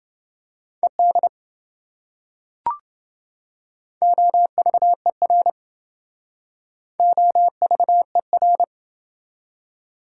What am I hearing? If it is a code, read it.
Morse code, heard as a keyed tone, about 27 wpm, 700 Hz. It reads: ED OVER OVER